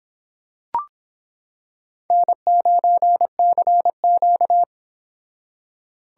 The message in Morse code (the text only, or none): N9CQ